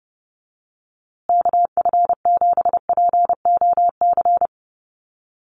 Morse code, heard as a keyed tone, about 30 wpm, 700 Hz. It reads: KF7POC
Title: USA Call Signs Repeated 2x 30wpm